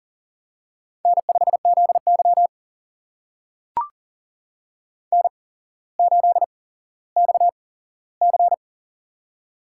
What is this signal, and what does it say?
Morse code, heard as a keyed tone, about 40 wpm, spaced as for 17 wpm, 700 Hz. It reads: N5ZY N8XC
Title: USA Call Signs - 4 Characters - 8x Character Spacing - 40wpm